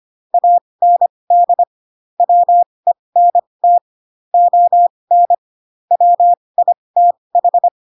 Morse code, heard as a keyed tone, about 25 wpm, spaced as for 20 wpm, 700 Hz. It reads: AND WENT ON WITH